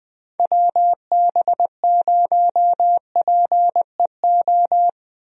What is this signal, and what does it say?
Morse code, heard as a keyed tone, about 20 wpm, 700 Hz. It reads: WB0PEO
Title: USA Call Signs - 6 Characters - Repeated 2x - 20wpm